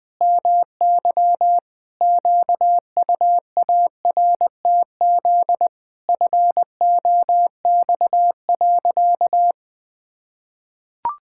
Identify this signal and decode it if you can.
Morse code, heard as a keyed tone, about 20 wpm, 700 Hz. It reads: MY QUARTZ FOX.
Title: Pangram Challenge v2 - 20wpm